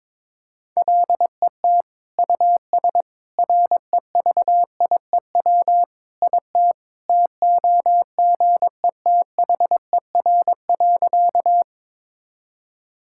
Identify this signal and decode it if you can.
Morse code, heard as a keyed tone, about 22 wpm, 700 Hz. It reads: LET US REVIEW IT TOGETHER.
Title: Sentences from Top 1000 Words - Review Part 6 - 22wpm